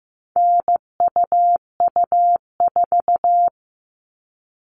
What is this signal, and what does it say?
Morse code, heard as a keyed tone, about 15 wpm, 700 Hz. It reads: NUU4
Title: All Cut Numbers - 4-Digit 15wpm